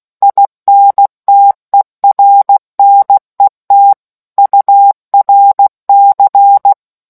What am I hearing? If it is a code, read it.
Morse code, heard as a keyed tone, about 16 wpm, 800 Hz. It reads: INTERNET URC